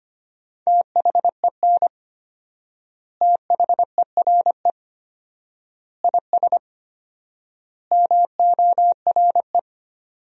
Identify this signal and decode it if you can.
Morse code, heard as a keyed tone, about 25 wpm, 700 Hz. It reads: THEN THERE IS MORE
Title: 4x Word-Spacing - Sentences from Top 100 Words - 25wpm